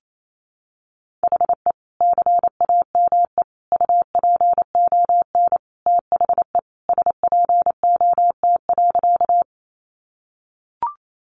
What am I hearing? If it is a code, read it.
Morse code, heard as a keyed tone, about 28 wpm, 700 Hz. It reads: HE CAME UPON THE SPOT.
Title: Sentences from Top 1100 Words 28wpm